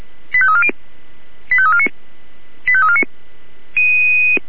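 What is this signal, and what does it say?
Five-tone selective call (ZVEI), heard as quick runs of five tones, tones of about 70 ms. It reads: 85340 85349 85349
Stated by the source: ZVEI five-tone call received by SDR.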